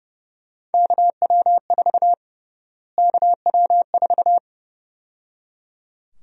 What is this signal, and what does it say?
Morse code, heard as a keyed tone, about 30 wpm, 700 Hz. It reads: KW4 KW4
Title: USA Call Sign Prefixes - Repeated 2x - 30wpm